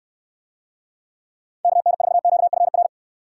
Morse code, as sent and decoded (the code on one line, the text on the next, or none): -.- -. ....- --.- ..-. .-.
KN4QFR